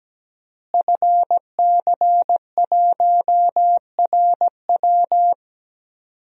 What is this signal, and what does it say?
Morse code, heard as a keyed tone, about 17 wpm, 700 Hz. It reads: FC1RW
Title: DXCC-100 Call Signs - 5 Characters - Repeated 3x - 17wpm